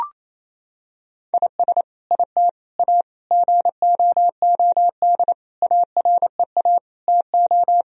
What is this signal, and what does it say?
Morse code, heard as a keyed tone, about 28 wpm, 700 Hz. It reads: IS IT A GOOD AREA TO
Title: Sentences from Top 1000 Words - Review Part 2 - 28wpm